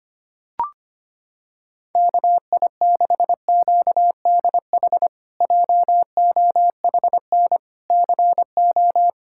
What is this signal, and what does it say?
Morse code, heard as a keyed tone, about 25 wpm, 700 Hz. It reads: KI6QDH JOHN CO
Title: K1USN SST QSO Practice - 25wpm